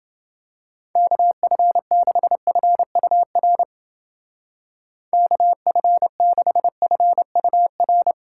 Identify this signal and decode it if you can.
Morse code, heard as a keyed tone, about 29 wpm, 700 Hz. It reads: KF6FUR KF6FUR